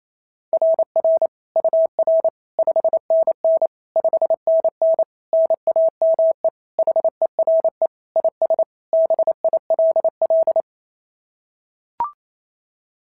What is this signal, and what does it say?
Morse code, heard as a keyed tone, about 28 wpm, 690 Hz. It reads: RR UR 5NN 5NN NAME HERE IS BILL
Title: SOTA QSO Practice 28wpm